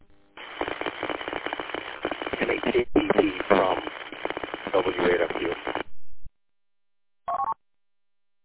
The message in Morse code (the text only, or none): none